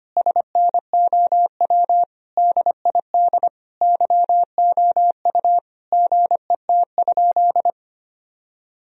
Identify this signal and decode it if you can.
Morse code, heard as a keyed tone, about 25 wpm, 700 Hz. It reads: SNOW DID YOU GET?
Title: Sentences from Top 1000 Words - Review Part 3 - 25wpm